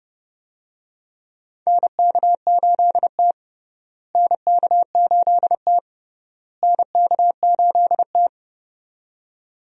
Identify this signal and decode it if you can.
Morse code, heard as a keyed tone, about 32 wpm, 700 Hz. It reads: NK8T NK8T NK8T
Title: USA Call Signs - 4 Characters - Repeated 3x - 30wpm